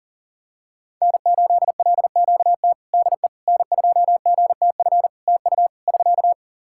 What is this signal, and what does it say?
Morse code, heard as a keyed tone, about 40 wpm, 700 Hz. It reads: N8LQT DE N2GTF TU <SK>
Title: K1USN SST QSO Practice 40wpm